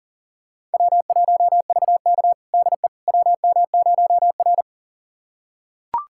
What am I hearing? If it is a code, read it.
Morse code, heard as a keyed tone, about 37 wpm, 700 Hz. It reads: W1VK DE WM0R
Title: USA Callsign DE Callsign - 4 Characters - 40wpm